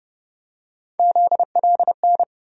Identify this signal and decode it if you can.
Morse code, heard as a keyed tone, about 30 wpm, 700 Hz. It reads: ZLN
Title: USA Call Sign Postfixes - 3 Letters 30wpm